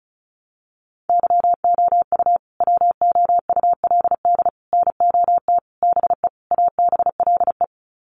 Morse code, heard as a keyed tone, about 34 wpm, 700 Hz. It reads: YOU WOULD NOT BE ABLE
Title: Sets of 5 Words Encore2 35wpm